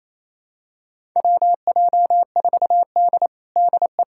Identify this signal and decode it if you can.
Morse code, heard as a keyed tone, about 28 wpm, 700 Hz. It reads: WJ4D DE
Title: USA Callsign DE Callsign - 4 Characters - 28wpm